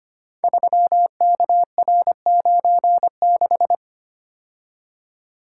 Morse code, heard as a keyed tone, about 25 wpm, 700 Hz. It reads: 3KR96
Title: Code Groups - 5 Letter:Numbers - 25wpm